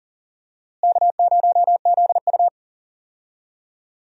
Morse code, heard as a keyed tone, about 40 wpm, 700 Hz. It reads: K0ZU
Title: USA Call Signs - 4 Characters - Repeated 2x - 40wpm